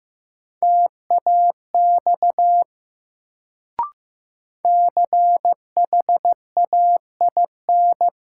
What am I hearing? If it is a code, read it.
Morse code, heard as a keyed tone, about 15 wpm, 700 Hz. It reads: TAX CHAIN
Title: Sentences from Top 1500 Words - New Word Drill Alt - 15wpm